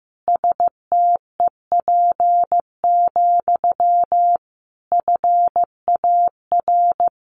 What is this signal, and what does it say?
Morse code, heard as a keyed tone, about 15 wpm, 700 Hz. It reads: STEP, FAR
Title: A Sound Of Thunder 15wpm